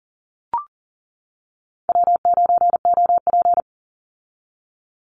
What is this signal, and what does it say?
Morse code, heard as a keyed tone, about 41 wpm, 700 Hz. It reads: W9OP
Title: Call Signs - 40wpm